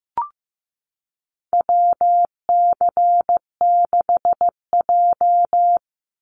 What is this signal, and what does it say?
Morse code, heard as a keyed tone, about 15 wpm, 700 Hz. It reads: WC6J
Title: NA SOTA Chaser Callsigns 15wpm